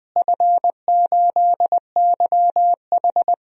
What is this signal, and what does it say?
Morse code, heard as a keyed tone, about 20 wpm, 700 Hz. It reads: F8YH
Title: DXCC-100 Call Signs - 4 Characters - Repeated 3x - 20wpm